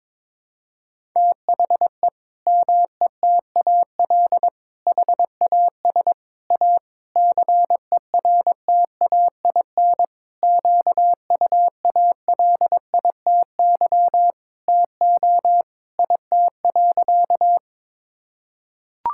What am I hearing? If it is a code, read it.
Morse code, heard as a keyed tone, about 22 wpm, 700 Hz. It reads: THE METAL HAS A CERTAIN QUALITY TO IT.
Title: Sentences from Top 1000 Words - Review Part 6 - 22wpm